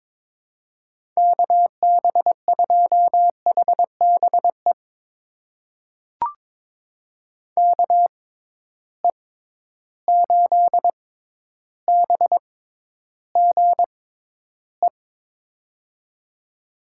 Morse code, heard as a keyed tone, about 22 wpm, spaced as for 11 wpm, 700 Hz. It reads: KB2HBE KE8BGE
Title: USA Call Signs - 6 Characters - 6x Character Spacing - 22wpm